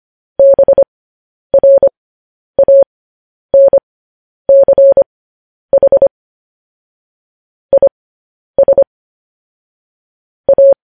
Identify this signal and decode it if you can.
Morse code, heard as a keyed tone, about 25 wpm, spaced as for 10 wpm, 550 Hz. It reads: BRANCH IS A
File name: podcast / News headlines-1x-25x10